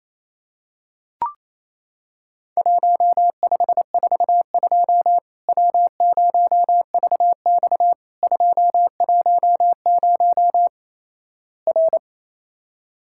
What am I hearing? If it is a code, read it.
Morse code, heard as a keyed tone, about 28 wpm, 700 Hz. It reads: 1542 W0VX 210 R
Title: Worked All Europe DX Contest - QTC Exchange - 28wpm